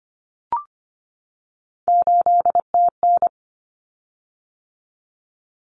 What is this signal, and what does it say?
Morse code, heard as a keyed tone, about 25 wpm, 700 Hz. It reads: 8TN
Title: Common Cut Numbers - 3-Digit 25wpm